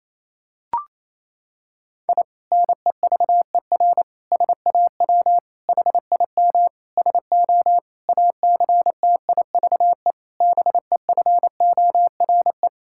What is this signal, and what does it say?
Morse code, heard as a keyed tone, about 28 wpm, 700 Hz. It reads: I NEVER SAW HIM SO ACTIVE BEFORE
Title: Sentences from Top 1000 Words - Review Part 6 - 28wpm